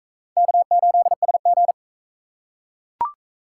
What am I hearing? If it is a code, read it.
Morse code, heard as a keyed tone, about 36 wpm, 700 Hz. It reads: K8SG